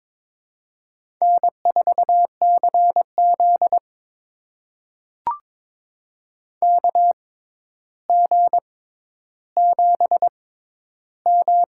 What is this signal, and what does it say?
Morse code, heard as a keyed tone, about 22 wpm, spaced as for 11 wpm, 700 Hz. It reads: N4CZ KG7M
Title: USA Call Signs - 6x Character Spacing - 22wpm